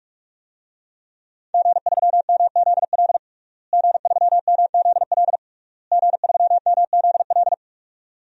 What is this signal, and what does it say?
Morse code, heard as a keyed tone, about 46 wpm, 700 Hz. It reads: G3MZL G3MZL G3MZL
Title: DXCC-100 Call Signs - 5 Characters - Repeated 3x - 45wpm